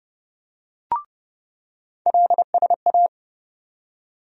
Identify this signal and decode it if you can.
Morse code, heard as a keyed tone, about 32 wpm, 700 Hz. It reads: LSA
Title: USA Call Sign Postfixes - 3 Letters 30wpm